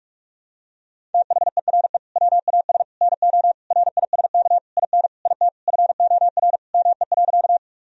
Morse code, heard as a keyed tone, about 41 wpm, 700 Hz. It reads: THERE WAS NO RISK IN IT FOR ME.
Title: Sentences from Top 1000 Words - Review Part 4 - 45wpm